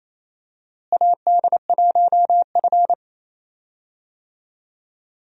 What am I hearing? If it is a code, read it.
Morse code, heard as a keyed tone, about 28 wpm, 700 Hz. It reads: AD1F
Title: USA Call Signs - 4 Characters - Repeated 3x - 28wpm